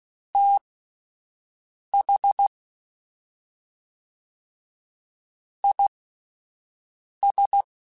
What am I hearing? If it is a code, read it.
Morse code, heard as a keyed tone, about 16 wpm, spaced as for 4 wpm, 800 Hz. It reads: THIS